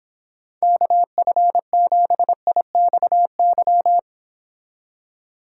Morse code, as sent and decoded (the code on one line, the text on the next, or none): -.- ..-. --... .. -..- -.--
KF7IXY